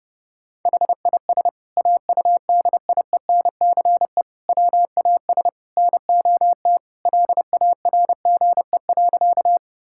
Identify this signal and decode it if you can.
Morse code, heard as a keyed tone, about 32 wpm, 700 Hz. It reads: HIS AUDIENCE WAS NOT LARGE.
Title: Sentences from Top 1000 Words - Review Part 5 - 30wpm